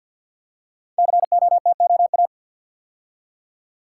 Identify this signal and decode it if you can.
Morse code, heard as a keyed tone, about 50 wpm, 700 Hz. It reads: COTOA
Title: Code Groups - 5 Letters - 50wpm